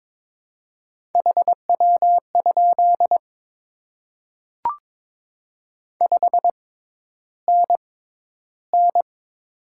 Morse code, heard as a keyed tone, about 22 wpm, spaced as for 11 wpm, 700 Hz. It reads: HW? 5NN
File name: Top 20 QSO Elements - 6x Character Spacing - 22wpm